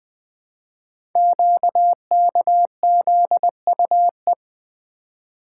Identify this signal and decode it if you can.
Morse code, heard as a keyed tone, about 20 wpm, 700 Hz. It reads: QKZUE
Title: Code Groups - 5 Letters - 20wpm